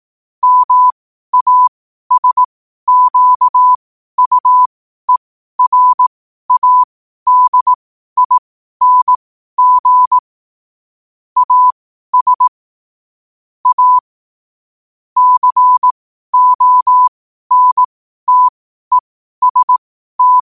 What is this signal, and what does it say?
Morse code, heard as a keyed tone, about 18 wpm, spaced as for 12 wpm, 1000 Hz. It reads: MASQUERADING AS A CONTEST
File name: qotdfiles / 12wpm